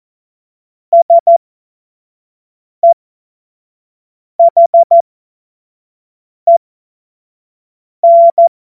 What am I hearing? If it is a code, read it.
Morse code, heard as a keyed tone, about 14 wpm, spaced as for 5 wpm, 680 Hz. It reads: SEHEN
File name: Morsekurs / Witze / Witz 13 Mai 2025 25 F70